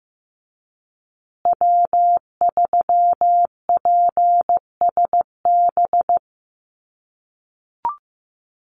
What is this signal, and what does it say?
Morse code, heard as a keyed tone, about 15 wpm, 700 Hz. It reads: W3PSB
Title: USA Call Signs Repeated 2x 15wpm